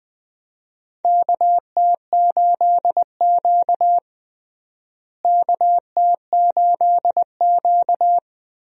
Morse code, heard as a keyed tone, about 20 wpm, 700 Hz. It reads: KT8Q KT8Q
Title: USA Call Signs - 4 Characters - Repeated 2x - 20wpm